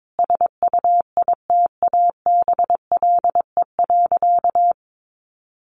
Morse code, heard as a keyed tone, about 22 wpm, 700 Hz. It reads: SUITABLE.
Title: Sentences from Top 1400 Words 22wpm